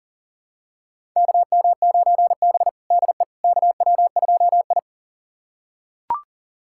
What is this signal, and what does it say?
Morse code, heard as a keyed tone, about 40 wpm, 700 Hz. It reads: KM9B DE KW2I
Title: USA Callsign DE Callsign - 4 Characters - 40wpm